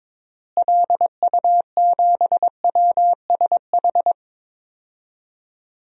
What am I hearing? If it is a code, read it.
Morse code, heard as a keyed tone, about 22 wpm, 700 Hz. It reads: LU7WSH
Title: DXCC-100 Call Signs - 6 Characters - Repeated 3x - 22wpm